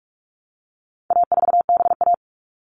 Morse code, heard as a keyed tone, about 44 wpm, 700 Hz. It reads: A4BA
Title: All Cut Numbers - 4-Digit 45wpm